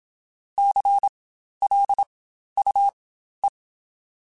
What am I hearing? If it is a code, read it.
Morse code, heard as a keyed tone, about 27 wpm, spaced as for 12 wpm, 800 Hz. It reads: CLUE